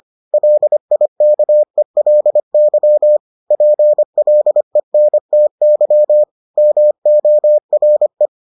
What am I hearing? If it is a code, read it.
Morse code, heard as a keyed tone, about 25 wpm, 600 Hz. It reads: LIKELY PLENTY MORE